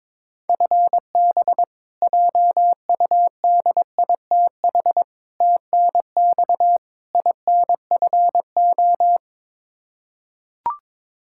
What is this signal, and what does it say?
Morse code, heard as a keyed tone, about 22 wpm, 700 Hz. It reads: FB JUDITH TNX INFO
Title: K1USN SST QSO Practice 22wpm